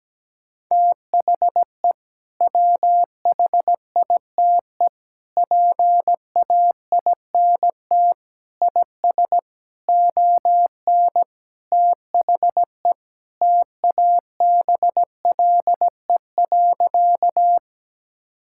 Morse code, heard as a keyed tone, about 17 wpm, 700 Hz. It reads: THE WHITE PAINT IS ON THE TABLE.
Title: Sentences from Top 1000 Words - Review Part 6 - 17wpm